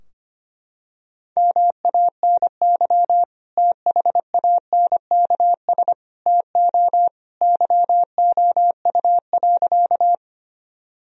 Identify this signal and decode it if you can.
Morse code, heard as a keyed tone, about 25 wpm, 700 Hz. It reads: MANY THANKS TO YOU.
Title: Sentences from Top 1000 Words - Review Part 3 - 25wpm